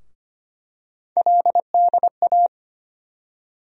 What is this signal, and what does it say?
Morse code, heard as a keyed tone, about 25 wpm, 700 Hz. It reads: LDA
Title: USA Call Sign Postfixes - 3 Letters 25wpm